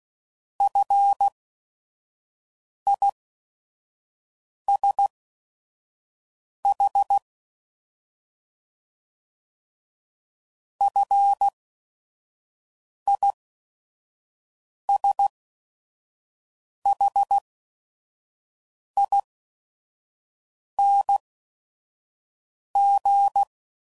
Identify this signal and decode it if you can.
Morse code, heard as a keyed tone, about 16 wpm, spaced as for 5 wpm, 800 Hz. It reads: FISH FISHING
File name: Prefixes / FLs & THs / Single Fs Ls